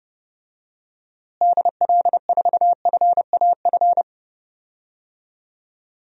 Morse code, heard as a keyed tone, about 30 wpm, 700 Hz. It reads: DL4FAF
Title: DXCC-100 Call Signs - 6 Characters - Repeated 3x - 30wpm